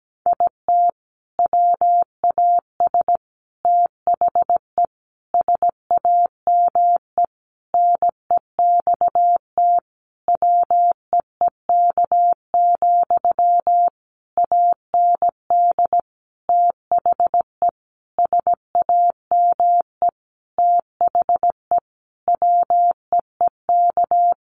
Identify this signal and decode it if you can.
Morse code, heard as a keyed tone, about 17 wpm, 700 Hz. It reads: IT WAS THE SAME NEXT WEEK, AND THE SAME THE WEEK
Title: Sentences from Top 500 Words 17wpm